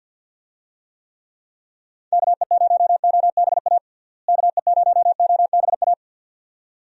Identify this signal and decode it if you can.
Morse code, heard as a keyed tone, about 52 wpm, 700 Hz. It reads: KE0OBA KE0OBA
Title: USA Call Signs - 6 Characters - Repeated 2x - 50wpm